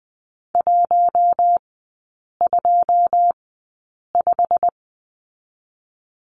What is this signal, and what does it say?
Morse code, heard as a keyed tone, about 20 wpm, spaced as for 8 wpm, 700 Hz. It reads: 125
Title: 178 Lesson 29e - Slash - Three Characters 20wpm